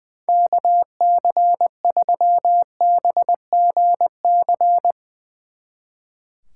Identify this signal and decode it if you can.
Morse code, heard as a keyed tone, about 21 wpm, 700 Hz. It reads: KC3BGC
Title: Speed Racing - USA Call Signs 20wpm